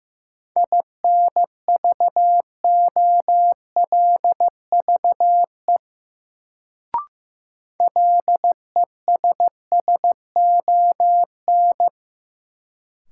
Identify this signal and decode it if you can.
Morse code, heard as a keyed tone, about 15 wpm, 700 Hz. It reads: INVOLVE LESSON
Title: Sentences from Top 1200 Words - New Word Drill - 15wpm